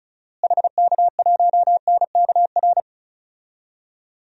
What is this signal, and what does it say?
Morse code, heard as a keyed tone, about 34 wpm, 700 Hz. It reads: HK1NKR
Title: DXCC-100 Call Signs - 6 Characters - Repeated 2x - 35wpm